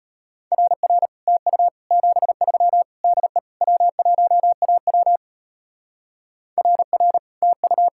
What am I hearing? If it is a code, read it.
Morse code, heard as a keyed tone, about 31 wpm, 700 Hz. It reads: RR TU 73 DE W1AW RR TU
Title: Speed-Racing Common QSO Phrases Encore - 25wpm